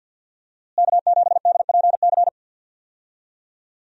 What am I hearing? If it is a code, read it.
Morse code, heard as a keyed tone, about 47 wpm, 700 Hz. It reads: K7DPC
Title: USA Call Signs - 5 Characters - Repeated 2x - 50wpm